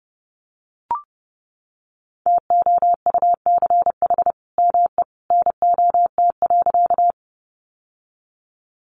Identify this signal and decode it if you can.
Morse code, heard as a keyed tone, about 30 wpm, 700 Hz. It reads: TOUCH ME NOT.
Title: Sentences from Top 1200 Words 30wpm